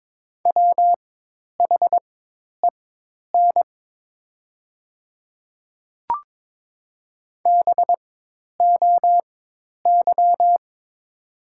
Morse code, heard as a keyed tone, about 22 wpm, spaced as for 10 wpm, 690 Hz. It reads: WHEN BOY
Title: 4x Character Spacing - Top 300 Words - 22wpm